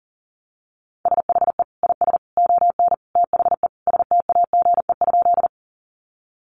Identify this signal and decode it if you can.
Morse code, heard as a keyed tone, about 40 wpm, 700 Hz. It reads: SHE IS ON THE STAGE?
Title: Sentences from Top 1200 Words 40wpm